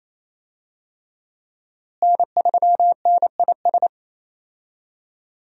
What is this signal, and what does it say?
Morse code, heard as a keyed tone, about 28 wpm, 700 Hz. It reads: N3NIS
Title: USA Call Signs - 5 Characters - Repeated 3x - 28wpm